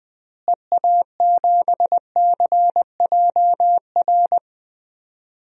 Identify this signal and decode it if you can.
Morse code, heard as a keyed tone, about 20 wpm, 700 Hz. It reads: EA7CJR
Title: DXCC-100 Call Signs - 6 Characters - Repeated 3x - 20wpm